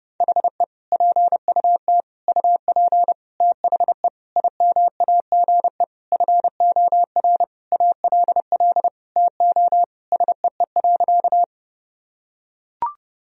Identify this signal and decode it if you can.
Morse code, heard as a keyed tone, about 30 wpm, 700 Hz. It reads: HE PUT UP THE IMAGE FOR ALL TO SEE.
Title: Sentences from Top 1000 Words - Review Part 4 - 30wpm